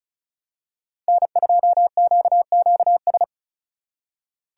Morse code, heard as a keyed tone, about 34 wpm, 700 Hz. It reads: N2QQS
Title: USA Call Signs - 5 Characters - 3x Character Spacing - 35wpm